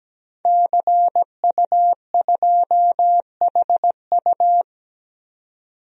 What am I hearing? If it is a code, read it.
Morse code, heard as a keyed tone, about 17 wpm, 700 Hz. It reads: CU2HU
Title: DXCC-100 Call Signs - 5 Characters - 17wpm